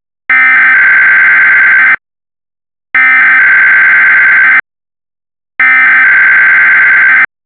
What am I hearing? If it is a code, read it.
Emergency Alert System (SAME) tones (header bursts), heard as harsh buzzing bursts. It is heard from 0.3 to 7.2 s.
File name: WXR-RWT